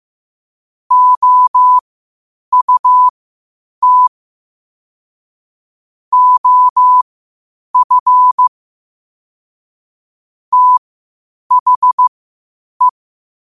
Morse code, heard as a keyed tone, about 15 wpm, spaced as for 8 wpm, 1000 Hz. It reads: OUT OF THE
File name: qotdfiles / 8wpm